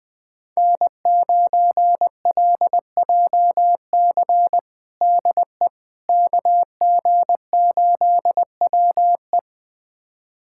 Morse code, heard as a keyed tone, about 21 wpm, 700 Hz. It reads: N9LJC DE KG8WE
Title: USA Callsign DE Callsign - 5 Characters - 20wpm